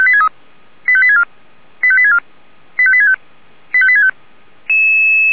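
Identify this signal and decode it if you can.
Five-tone selective call (ZVEI), heard as quick runs of five tones, tones of about 70 ms. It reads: 85864 85864 85865 85865
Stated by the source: ZVEI five-tone selective call received with an SDR.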